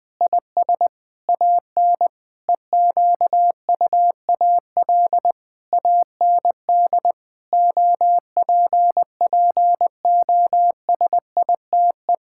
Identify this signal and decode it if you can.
Morse code, heard as a keyed tone, about 19 wpm, 700 Hz. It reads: IS AN EQUAL AND OPPOSITE
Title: Sentences with 8-Word Phrases - Spoken Beginning - 20wpm